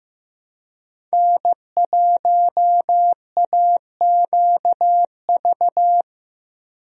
Morse code, heard as a keyed tone, about 15 wpm, 700 Hz. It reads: N1AQV
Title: USA Call Signs - 5 Characters - 15wpm